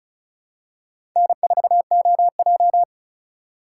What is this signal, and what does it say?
Morse code, heard as a keyed tone, about 36 wpm, 700 Hz. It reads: N4OJ
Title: USA Call Signs - 4 Characters - 4x Character Spacing - 35wpm